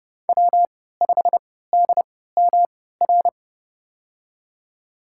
Morse code, heard as a keyed tone, about 32 wpm, spaced as for 17 wpm, 700 Hz. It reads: W5DMR
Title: USA Call Signs - 5 Characters - 3x Character Spacing - 30wpm